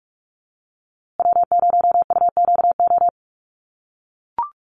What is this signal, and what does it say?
Morse code, heard as a keyed tone, about 44 wpm, 700 Hz. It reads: W0UQO
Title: USA Call Signs Repeated 2x 45wpm